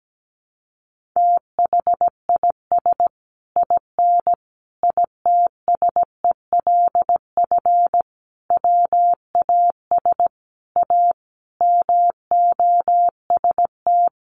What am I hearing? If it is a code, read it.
Morse code, heard as a keyed tone, about 17 wpm, 700 Hz. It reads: THIS IN ITSELF WAS A MOST
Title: Sentences from Top 1400 Words 17wpm